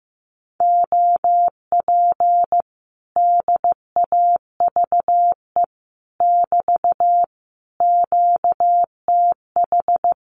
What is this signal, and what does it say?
Morse code, heard as a keyed tone, about 15 wpm, 700 Hz. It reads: OP DAVE <BT> QTH
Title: Common QSO Phrases Encore 15wpm